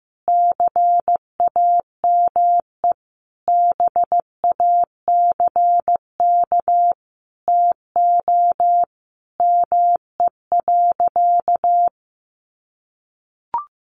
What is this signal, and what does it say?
Morse code, heard as a keyed tone, about 15 wpm, 700 Hz. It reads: CAME BACK TO ME.
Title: Sentences from Top 1100 Words 15wpm